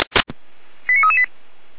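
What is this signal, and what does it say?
Five-tone selective call (ZVEI), heard as quick runs of five tones, tones of about 80 ms. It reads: 89338
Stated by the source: ZVEI five-tone selective call 89320 received via SDR.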